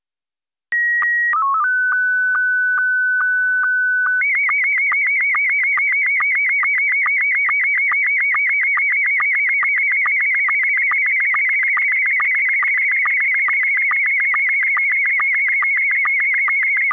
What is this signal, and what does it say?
SSTV (slow-scan television), Scottie 1.